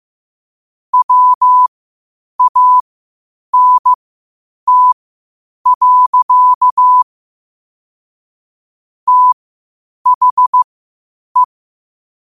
Morse code, heard as a keyed tone, about 15 wpm, spaced as for 8 wpm, 1000 Hz. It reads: WANT. THE
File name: qotdfiles / 8wpm